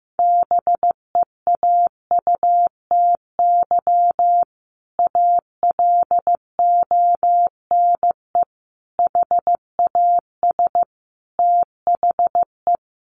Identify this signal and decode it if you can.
Morse code, heard as a keyed tone, about 15 wpm, 700 Hz. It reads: BEAUTY ALONE HAS THE
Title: Sentences from Top 1200 Words 15wpm